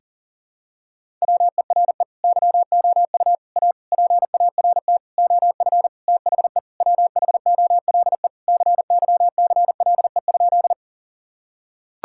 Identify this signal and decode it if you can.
Morse code, heard as a keyed tone, about 40 wpm, 700 Hz. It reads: WERE YOU A PART OF THE WHOLE CYCLE?
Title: Sentences from Top 1000 Words - Review Part 6 - 40wpm